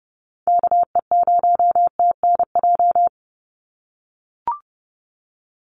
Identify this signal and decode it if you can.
Morse code, heard as a keyed tone, about 30 wpm, 700 Hz. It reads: KE0TNJ
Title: USA Call Signs 30wpm